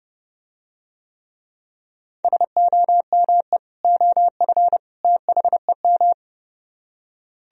Morse code, heard as a keyed tone, about 31 wpm, 700 Hz. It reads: SOME OF THEM